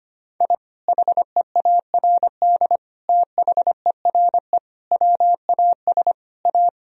Morse code, heard as a keyed tone, about 25 wpm, 700 Hz. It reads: I HEARD THERE WAS A
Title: Sentences from Top 1000 Words - Review Part 2 - 25wpm